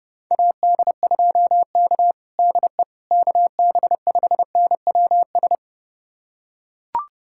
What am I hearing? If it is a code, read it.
Morse code, heard as a keyed tone, about 30 wpm, 700 Hz. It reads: AD2K DE KB5NWS
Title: USA Callsign DE Callsign 30wpm